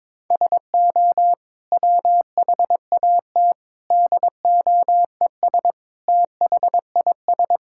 Morse code, heard as a keyed tone, about 22 wpm, 700 Hz. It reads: SO WHAT DOES THIS
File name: practice / 22wpm / Sets of 8 Words - 22wpm